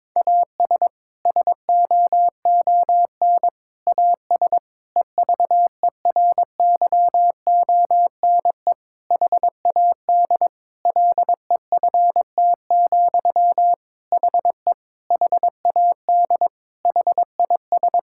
Morse code, heard as a keyed tone, about 22 wpm, 700 Hz. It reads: AS SOON AS EVERYONE HAD LEFT, HE HAD HIS
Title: Sentences from Top 1000 Words - Review Part 5 - 22wpm